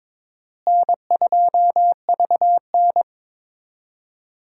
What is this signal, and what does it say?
Morse code, heard as a keyed tone, about 22 wpm, 700 Hz. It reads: N2VN
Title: USA Call Signs - 4 Characters - Repeated 2x - 22wpm